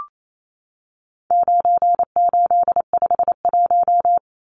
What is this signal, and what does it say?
Morse code, heard as a keyed tone, about 28 wpm, 700 Hz. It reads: 9851
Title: Numbers - 4-Digit 28wpm